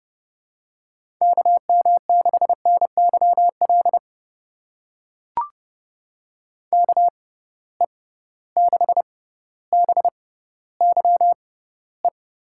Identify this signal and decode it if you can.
Morse code, heard as a keyed tone, about 30 wpm, spaced as for 15 wpm, 700 Hz. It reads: KM6NYL KE6BYE
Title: USA Call Signs - 6 Characters - 6x Character Spacing - 30wpm